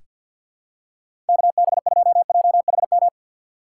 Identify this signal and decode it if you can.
Morse code, heard as a keyed tone, about 46 wpm, 700 Hz. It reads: KB2JHM
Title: USA Call Signs - 6 Characters - 6x Character Spacing - 50wpm